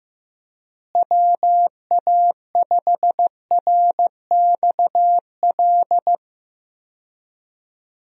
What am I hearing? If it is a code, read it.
Morse code, heard as a keyed tone, about 15 wpm, 700 Hz. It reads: WA5RXL
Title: USA Call Signs - 6 Characters - Repeated 3x - 15wpm